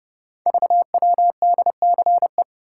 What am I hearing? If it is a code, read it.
Morse code, heard as a keyed tone, about 30 wpm, 700 Hz. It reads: VWDCE
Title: Code Groups - 5 Letters - 30wpm